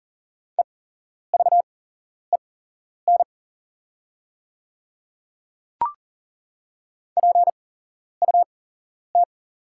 Morse code, heard as a keyed tone, about 41 wpm, spaced as for 11 wpm, 700 Hz. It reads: EVEN PUT
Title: 8x Character Spacing - Top 300 Words - 40wpm